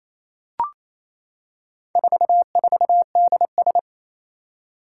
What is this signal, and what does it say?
Morse code, heard as a keyed tone, about 28 wpm, 700 Hz. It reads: 44DS
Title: Four Challenging Letter-Number Combinations 28wpm